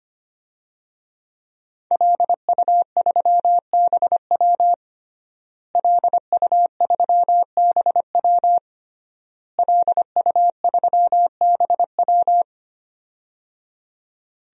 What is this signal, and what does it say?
Morse code, heard as a keyed tone, about 25 wpm, 700 Hz. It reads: LU3BW LU3BW LU3BW
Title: DXCC-100 Call Signs - 5 Characters - Repeated 3x - 25wpm